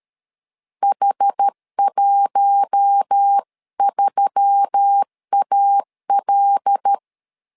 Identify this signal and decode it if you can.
Morse code, heard as a keyed tone, about 13 wpm, 800 Hz. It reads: H13AL
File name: LEZIONI / 5 LBVC / 5 LB 12wpm 1906